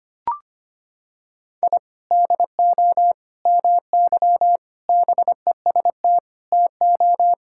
Morse code, heard as a keyed tone, about 25 wpm, 700 Hz. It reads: I DO MY BEST TO
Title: Sentences from Top 1000 Words - Review Part 6 - 25wpm